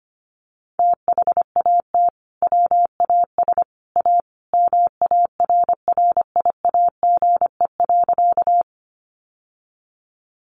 Morse code, heard as a keyed tone, about 25 wpm, 700 Hz. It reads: THAT WAS A MARRIAGE.
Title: Sentences from Top 800 Words 25wpm